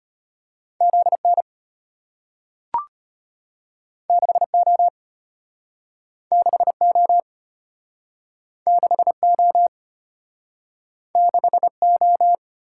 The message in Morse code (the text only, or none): ZN 6O 6O 6O 6O